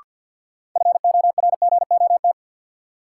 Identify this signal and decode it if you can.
Morse code, heard as a keyed tone, about 51 wpm, 700 Hz. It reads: FORGOT
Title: Sentences from Top 1400 Words - New Word Drill Alt - 50wpm